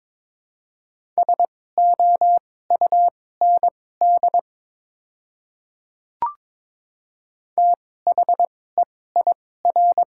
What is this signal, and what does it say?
Morse code, heard as a keyed tone, about 22 wpm, spaced as for 16 wpm, 700 Hz. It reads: SOUND THEIR
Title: 2x Character Spacing - Top 300 Words - 22wpm